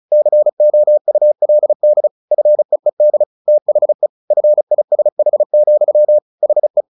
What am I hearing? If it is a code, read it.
Morse code, heard as a keyed tone, about 34 wpm, 600 Hz. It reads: COULD FEED THE FISH, HE
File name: the old man and the sea / the old man and the sea 1 35wpm